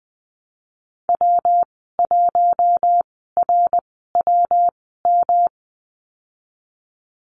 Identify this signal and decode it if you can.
Morse code, heard as a keyed tone, about 20 wpm, spaced as for 15 wpm, 700 Hz. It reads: W1RWM
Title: 128 Lesson 21g - Number 6 - Call Signs 20wpm